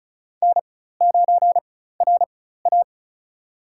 Morse code, heard as a keyed tone, about 34 wpm, spaced as for 16 wpm, 700 Hz. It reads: N9RA
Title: USA Call Signs - 4x Character Spacing - 35wpm